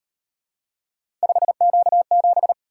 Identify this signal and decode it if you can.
Morse code, heard as a keyed tone, about 37 wpm, 700 Hz. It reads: 5Q7